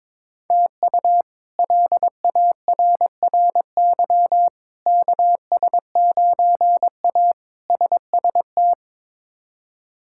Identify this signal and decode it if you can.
Morse code, heard as a keyed tone, about 22 wpm, 700 Hz. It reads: TU LARRY KS9A SST
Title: K1USN SST QSO Practice - 22wpm